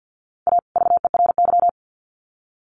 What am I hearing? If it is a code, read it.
Morse code, heard as a keyed tone, about 50 wpm, 700 Hz. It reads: A VERY
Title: Sets of 2 Words 50wpm